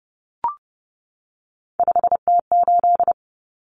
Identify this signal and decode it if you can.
Morse code, heard as a keyed tone, about 30 wpm, 700 Hz. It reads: 5T8
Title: Common Cut Numbers - 3-Digit 30wpm